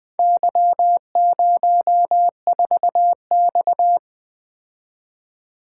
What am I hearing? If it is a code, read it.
Morse code, heard as a keyed tone, about 20 wpm, 700 Hz. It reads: Y04X
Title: Four Letter-Number Combinations - 20wpm